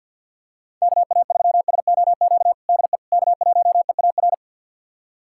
Morse code, heard as a keyed tone, about 52 wpm, 700 Hz. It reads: KA3SOQ DE K1EAR
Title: USA Callsign DE Callsign 50wpm